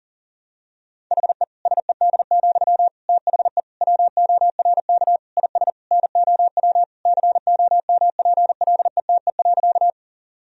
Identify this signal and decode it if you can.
Morse code, heard as a keyed tone, about 40 wpm, 700 Hz. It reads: HE SED, THE WORK IS NOW COMPLETE.
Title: Sentences from Top 400 Words and Abbreviations 40wpm